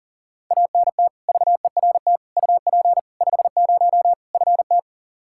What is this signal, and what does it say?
Morse code, heard as a keyed tone, about 40 wpm, 700 Hz. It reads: ANT VERT UP 50 FT
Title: Practice QSO - Exchange 2 - Info - 40wpm